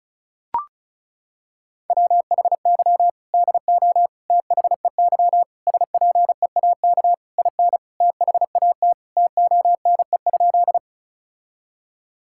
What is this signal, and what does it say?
Morse code, heard as a keyed tone, about 34 wpm, 700 Hz. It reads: WHY DO THEY SPEAK IN THAT TONE?
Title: Sentences from Top 1600 Words 35wpm